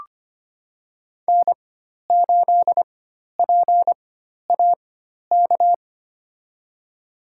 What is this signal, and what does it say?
Morse code, heard as a keyed tone, about 25 wpm, spaced as for 12 wpm, 700 Hz. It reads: N8PAK
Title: USA Call Signs - 5 Characters - 4x Character Spacing - 25wpm